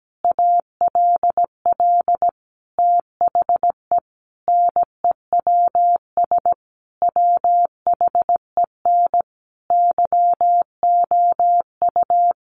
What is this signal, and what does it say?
Morse code, heard as a keyed tone, about 17 wpm, 700 Hz. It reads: ALL THE NEWS WHEN YOU
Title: Sentences from Top 800 Words 17wpm